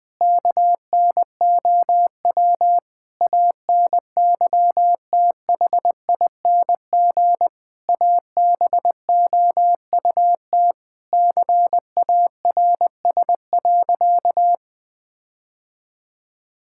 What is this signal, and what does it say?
Morse code, heard as a keyed tone, about 20 wpm, 700 Hz. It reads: KNOW ANYTHING ABOUT CARS.
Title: Sentences with 5-Word Phrases - 20wpm